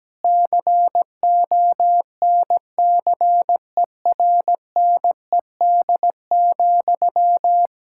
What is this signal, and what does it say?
Morse code, heard as a keyed tone, about 17 wpm, 700 Hz. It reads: CONCERNED,
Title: Sentences from Top 1000 Words - Review Part 6 - 17wpm